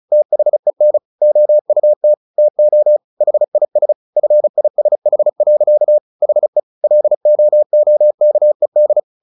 Morse code, heard as a keyed tone, about 34 wpm, 600 Hz. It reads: THEN OUT TO HIS FISH. HE LOOKED
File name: the old man and the sea 2 35wpm